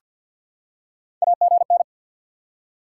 Morse code, heard as a keyed tone, about 50 wpm, 700 Hz. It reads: AGN
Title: Top 90 QSO Elements 50wpm